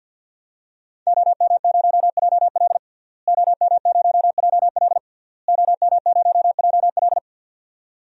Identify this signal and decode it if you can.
Morse code, heard as a keyed tone, about 53 wpm, 700 Hz. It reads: OM0JL OM0JL OM0JL
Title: DXCC-100 Call Signs - 5 Characters - Repeated 3x - 50wpm